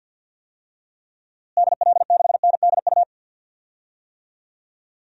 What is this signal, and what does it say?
Morse code, heard as a keyed tone, about 46 wpm, 700 Hz. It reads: DL6NDU
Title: DXCC-100 Call Signs - 6 Characters - Repeated 2x - 50wpm